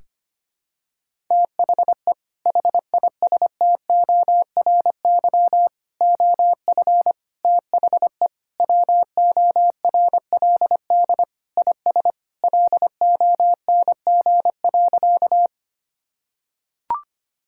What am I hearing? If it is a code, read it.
Morse code, heard as a keyed tone, about 25 wpm, 700 Hz. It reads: THE HISTORY OF THE WORLD IS LONG.
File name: practice / 25wpm / Sentences with 3-Word Phrases - 25wpm